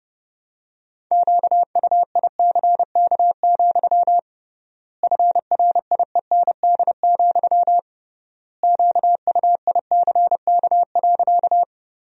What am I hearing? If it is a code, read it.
Morse code, heard as a keyed tone, about 32 wpm, 700 Hz. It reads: QUICK, FRIEND, QUICK.
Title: 3x Word-Spacing - Sentences from Top 500 Words - 30wpm